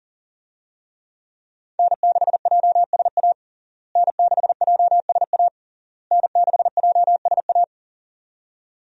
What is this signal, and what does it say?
Morse code, heard as a keyed tone, about 40 wpm, 700 Hz. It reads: N6JSA N6JSA N6JSA
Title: USA Call Signs - 5 Characters - Repeated 3x - 40wpm